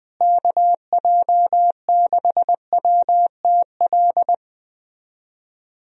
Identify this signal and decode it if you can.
Morse code, heard as a keyed tone, about 20 wpm, 700 Hz. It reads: KJ6WTL
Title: USA Call Signs - 6 Characters - Repeated 3x - 20wpm